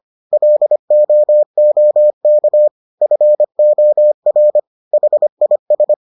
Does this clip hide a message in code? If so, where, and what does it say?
Yes: Morse code from 0.3 to 5.9 s, heard as a keyed tone, about 25 wpm, 600 Hz. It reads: LOOK FOR HIS